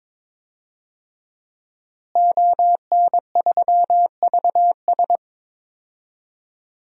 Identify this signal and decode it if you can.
Morse code, heard as a keyed tone, about 21 wpm, 700 Hz. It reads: ON3VS
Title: DXCC-100 Call Signs - 5 Characters - Repeated 3x - 22wpm